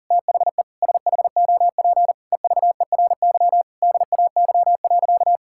Morse code, heard as a keyed tone, about 40 wpm, 700 Hz. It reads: THE SHOP EVERY DAY.
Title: Sentences from Top 1600 Words 40wpm